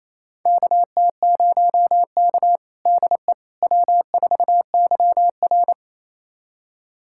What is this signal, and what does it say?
Morse code, heard as a keyed tone, about 27 wpm, 690 Hz. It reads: KT0K DE W4YR
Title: USA Callsign DE Callsign - 4 Characters - 28wpm